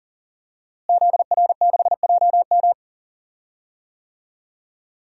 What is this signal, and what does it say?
Morse code, heard as a keyed tone, about 40 wpm, 700 Hz. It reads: ZR6JM
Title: DXCC-100 Call Signs - 5 Characters - Repeated 2x - 40wpm